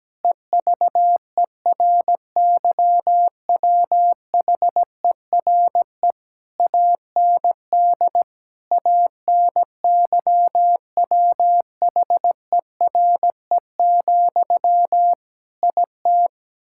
Morse code, heard as a keyed tone, about 17 wpm, 700 Hz. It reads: EVERYWHERE AND ANYWHERE, IT
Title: Sentences from Top 1600 Words 17wpm